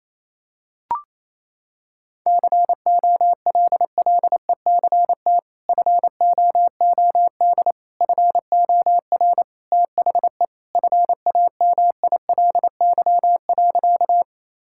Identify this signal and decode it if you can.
Morse code, heard as a keyed tone, about 28 wpm, 700 Hz. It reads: COLLECT FOOD FOR THE FAMILY.
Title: Sentences from Top 1600 Words 28wpm